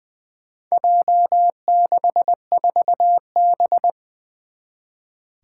Morse code, heard as a keyed tone, about 20 wpm, 700 Hz. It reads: J64B
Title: Four Challenging Letter-Number Combinations 20wpm